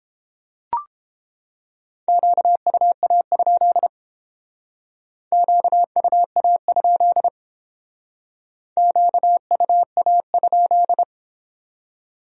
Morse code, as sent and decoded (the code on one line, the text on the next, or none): --.- ..- .- ..--.. / --.- ..- .- ..--.. / --.- ..- .- ..--..
QUA? QUA? QUA?